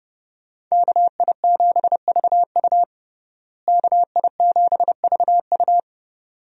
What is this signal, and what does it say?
Morse code, heard as a keyed tone, about 30 wpm, 700 Hz. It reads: KI7VU KI7VU
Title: USA Call Signs - 5 Characters - Repeated 2x - 30wpm